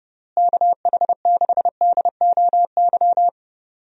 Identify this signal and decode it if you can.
Morse code, heard as a keyed tone, about 30 wpm, 700 Hz. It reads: KH6DOY
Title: DXCC-100 Call Signs - 6 Characters - Repeated 2x - 30wpm